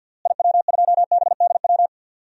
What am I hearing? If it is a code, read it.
Morse code, heard as a keyed tone, about 50 wpm, 700 Hz. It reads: IW2BDW
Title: DXCC-100 Call Signs - 6 Characters - Repeated 3x - 50wpm